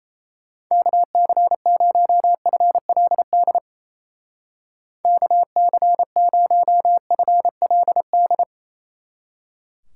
Morse code, heard as a keyed tone, about 34 wpm, 700 Hz. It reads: KC0FLD KC0FLD
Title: Speed Racing - USA 2x3 Call Signs 28wpm